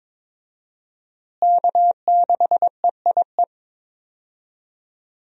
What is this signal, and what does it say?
Morse code, heard as a keyed tone, about 22 wpm, 700 Hz. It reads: K6EIE
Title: USA Call Signs - 5 Characters - Repeated 3x - 22wpm